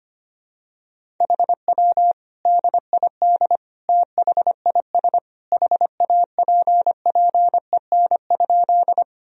Morse code, heard as a keyed tone, about 25 wpm, 700 Hz. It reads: HW DID THIS HAPPEN?
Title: Sentences from Top 400 Words and Abbreviations 25wpm